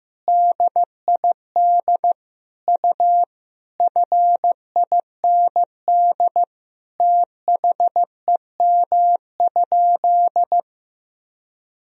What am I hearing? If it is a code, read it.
Morse code, heard as a keyed tone, about 15 wpm, 700 Hz. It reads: DID U FIND THEM?
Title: Sentences from Top 200 Words and Abbreviations 15wpm